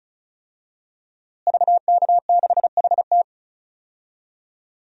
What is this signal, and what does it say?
Morse code, heard as a keyed tone, about 36 wpm, 700 Hz. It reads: VK6HT
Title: DXCC-100 Call Signs - 5 Characters - Repeated 3x - 35wpm